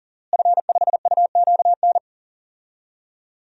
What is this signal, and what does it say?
Morse code, heard as a keyed tone, about 37 wpm, 700 Hz. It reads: F5UQN
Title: DXCC-100 Call Signs - 5 Characters - Repeated 3x - 40wpm